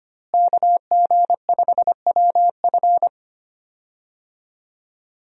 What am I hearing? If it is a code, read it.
Morse code, heard as a keyed tone, about 25 wpm, 700 Hz. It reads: KG5WF
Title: USA Call Signs - 5 Characters - Repeated 3x - 25wpm